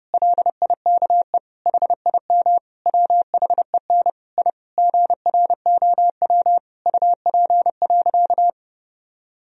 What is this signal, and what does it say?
Morse code, heard as a keyed tone, about 32 wpm, 700 Hz. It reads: LIKE HIM WHEN I GROW UP.
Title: Sentences from Top 1000 Words - Review Part 1 - 30wpm